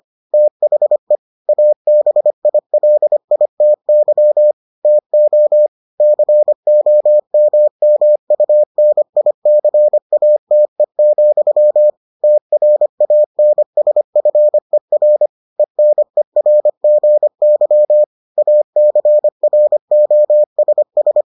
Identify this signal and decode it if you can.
Morse code, heard as a keyed tone, about 25 wpm, 600 Hz. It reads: THE ABILITY TO COMMUNICATE, TRANSFER ENERGY ACROSS